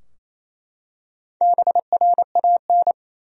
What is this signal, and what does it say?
Morse code, heard as a keyed tone, about 27 wpm, 700 Hz. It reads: BRAN
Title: USA Common Names - Max 4 Letters - 28wpm